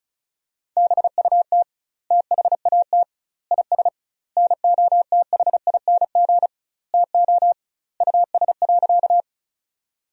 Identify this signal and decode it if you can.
Morse code, heard as a keyed tone, about 36 wpm, 700 Hz. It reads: BUT THAT IS NOTHING TO US.
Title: 2x Word-Spacing - Sentences from Top 500 Words - 35wpm